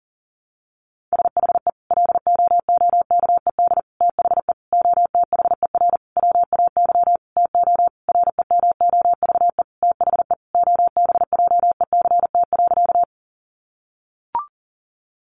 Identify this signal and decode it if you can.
Morse code, heard as a keyed tone, about 40 wpm, 700 Hz. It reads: SHE LOOKED THE OTHER WAY TO REMOVE THE OBJECT.
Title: Sentences from Top 800 Words 40wpm